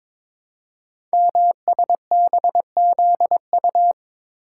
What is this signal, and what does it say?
Morse code, heard as a keyed tone, about 23 wpm, 700 Hz. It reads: MSBZU
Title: Code Groups - 5 Letters - 22wpm